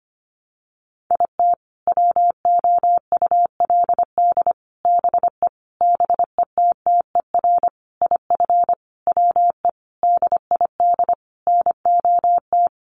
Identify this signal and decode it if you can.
Morse code, heard as a keyed tone, about 25 wpm, 700 Hz. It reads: IT WOULD BE BETTER IF WE DID NOT
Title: Sentences from Top 1100 Words 25wpm